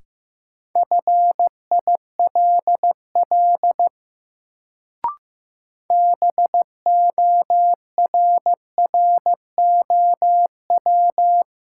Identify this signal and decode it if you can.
Morse code, heard as a keyed tone, about 15 wpm, 700 Hz. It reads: FILL BORROW
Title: Sentences from Top 1500 Words - New Word Drill Alt - 15wpm